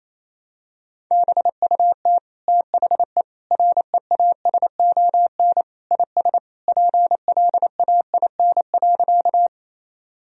Morse code, heard as a keyed tone, about 27 wpm, 700 Hz. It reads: BUT THE REASON IS PLAIN.
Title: Sentences from Top 1000 Words - Review Part 2 - 28wpm